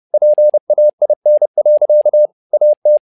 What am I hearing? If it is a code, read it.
Morse code, heard as a keyed tone, about 30 wpm, 600 Hz. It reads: PAIN. AT
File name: learning center / tower trouble / tower trouble 30wpm